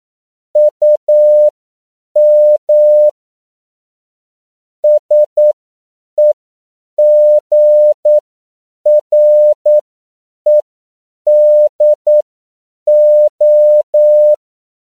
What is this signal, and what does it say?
Morse code, heard as a keyed tone, about 9 wpm, spaced as for 7 wpm, 600 Hz. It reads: UM SEGREDO
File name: CW / 10PPM